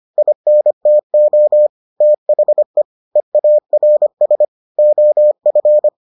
Morse code, heard as a keyed tone, about 25 wpm, 600 Hz. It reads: INTO THE EARS OF